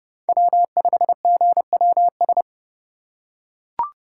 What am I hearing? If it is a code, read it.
Morse code, heard as a keyed tone, about 30 wpm, 700 Hz. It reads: W5GWS
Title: USA Call Signs - 5 Characters - 3x Character Spacing - 30wpm